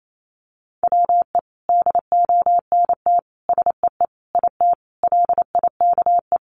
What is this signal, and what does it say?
Morse code, heard as a keyed tone, about 28 wpm, 700 Hz. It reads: WE DONT SEE IT LIKE
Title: Sentences from Top 300 Words 28wpm